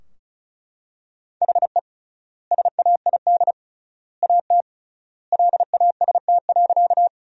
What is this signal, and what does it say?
Morse code, heard as a keyed tone, about 36 wpm, 700 Hz. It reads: HE SAID AT LAST.
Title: 3x Word-Spacing - Sentences from Top 300 Words - 35wpm